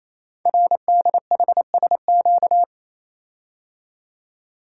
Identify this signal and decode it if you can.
Morse code, heard as a keyed tone, about 27 wpm, 700 Hz. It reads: RDHSQ
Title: Code Groups - 5 Letters - 28wpm